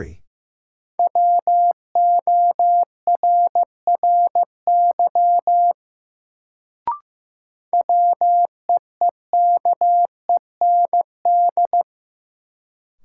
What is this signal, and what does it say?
Morse code, heard as a keyed tone, about 16 wpm, 700 Hz. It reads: WORRY WEEKEND
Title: Sets of 5 Words - New Word Drill - 15wpm